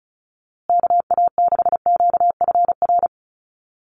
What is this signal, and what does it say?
Morse code, heard as a keyed tone, about 34 wpm, 700 Hz. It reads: KA6QFR
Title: USA Call Signs Repeated 3x 35wpm